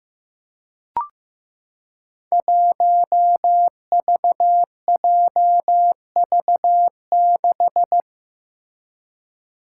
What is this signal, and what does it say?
Morse code, heard as a keyed tone, about 15 wpm, 700 Hz. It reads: 1VJV6
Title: Five Challenging Letter-Number Combinations 15wpm